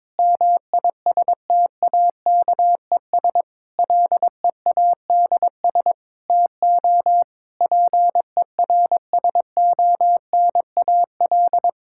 Morse code, heard as a keyed tone, about 22 wpm, 700 Hz. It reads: MISTAKES LEADS TO PERSONAL
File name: Difficult Sentences with Hints - 22wpm